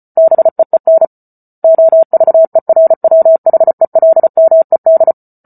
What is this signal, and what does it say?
Morse code, heard as a keyed tone, about 34 wpm, 650 Hz. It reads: BEEN OVERWHELMED